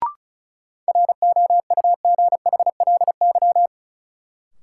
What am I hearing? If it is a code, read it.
Morse code, heard as a keyed tone, about 34 wpm, 700 Hz. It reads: ROUGHLY
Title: Sentences from Top 1500 Words - New Word Drill - 35wpm